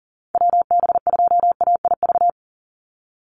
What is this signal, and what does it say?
Morse code, heard as a keyed tone, about 40 wpm, 700 Hz. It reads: WB2AIV
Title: Call Signs Repeated 2x - 40wpm